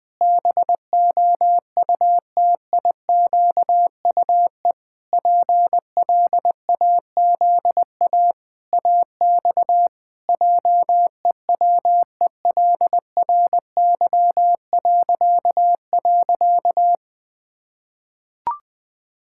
Morse code, heard as a keyed tone, about 20 wpm, 700 Hz. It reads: BOUTIQUE PLAZA AX JEWELRY..
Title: Pangram Challenge v2 - 20wpm